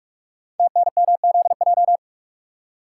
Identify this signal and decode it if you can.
Morse code, heard as a keyed tone, about 44 wpm, 690 Hz. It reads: TNMZJ
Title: Code Groups - 5 Letters - 45wpm